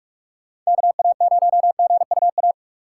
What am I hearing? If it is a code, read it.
Morse code, heard as a keyed tone, about 43 wpm, 700 Hz. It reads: KA0GUA
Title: USA Call Signs - 6 Characters - 4x Character Spacing - 45wpm